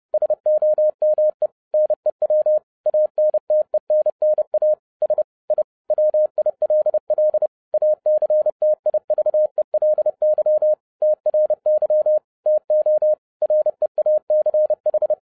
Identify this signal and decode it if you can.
Morse code, heard as a keyed tone, about 30 wpm, 610 Hz. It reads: SOME NEW ANTENNAS I WILL ACTIVELY TRY TO REACH